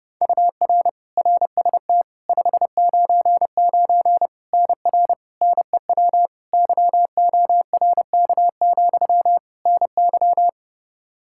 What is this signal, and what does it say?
Morse code, heard as a keyed tone, about 32 wpm, 700 Hz. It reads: UR RST 599 NR NEW YORK, NY
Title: K1USN SST QSO Practice 30wpm